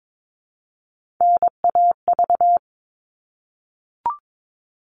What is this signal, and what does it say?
Morse code, heard as a keyed tone, about 22 wpm, 700 Hz. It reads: NAV
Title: All Cut Numbers - 3-Digit 22wpm